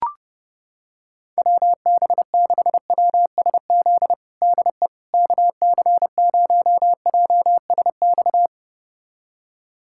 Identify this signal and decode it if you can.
Morse code, heard as a keyed tone, about 30 wpm, 700 Hz. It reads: WB6WSZ DE KC0JSX
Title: USA Callsign DE Callsign 30wpm